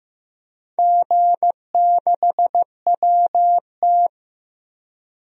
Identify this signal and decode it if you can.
Morse code, heard as a keyed tone, about 15 wpm, 700 Hz. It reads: G6WT
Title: DXCC-100 Call Signs - 4 Characters - Repeated 2x - 15wpm